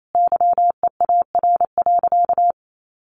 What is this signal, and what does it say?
Morse code, heard as a keyed tone, about 28 wpm, 700 Hz. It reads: YEAR.
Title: Sentences from Top 1400 Words 28wpm